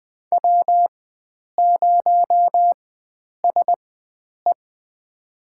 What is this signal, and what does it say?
Morse code, heard as a keyed tone, about 21 wpm, spaced as for 9 wpm, 700 Hz. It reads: W0SE
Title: USA Call Signs - 4 Characters - 4x Character Spacing - 20wpm